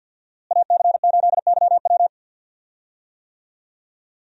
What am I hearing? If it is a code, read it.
Morse code, heard as a keyed tone, about 51 wpm, 700 Hz. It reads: AC8YW
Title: USA Call Signs - 5 Characters - Repeated 2x - 50wpm